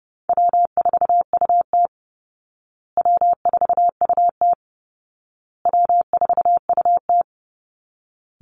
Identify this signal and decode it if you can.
Morse code, heard as a keyed tone, about 30 wpm, 700 Hz. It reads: W4UT W4UT W4UT
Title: Easy Call Signs Repeated 3x - 30wpm